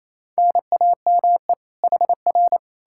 Morse code, heard as a keyed tone, about 37 wpm, 700 Hz. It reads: NAME HR